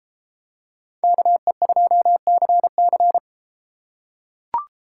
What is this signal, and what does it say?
Morse code, heard as a keyed tone, about 31 wpm, 700 Hz. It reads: KE2CC